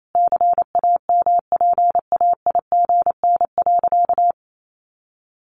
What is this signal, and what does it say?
Morse code, heard as a keyed tone, about 28 wpm, 700 Hz. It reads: CAMPAIGN.
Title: Sentences from Top 1200 Words 28wpm